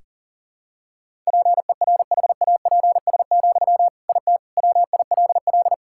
Morse code, heard as a keyed tone, about 40 wpm, 700 Hz. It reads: PERHAPS, IT WILL
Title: Sentences from Top 1000 Words - Review Part 6 - 40wpm